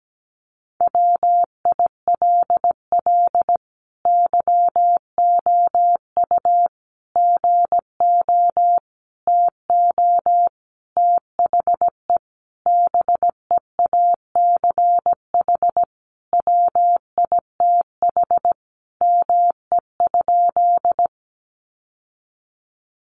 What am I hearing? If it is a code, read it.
Morse code, heard as a keyed tone, about 17 wpm, 700 Hz. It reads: WILL YOU GO TO THE BEACH WITH ME?
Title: Sentences from Top 1500 Words 17wpm